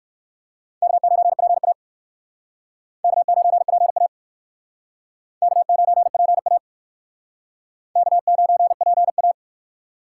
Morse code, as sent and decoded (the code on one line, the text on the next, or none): -.- ----. .--. .- / -.- ----. .--. .- / -.- ----. .--. .- / -.- ----. .--. .-
K9PA K9PA K9PA K9PA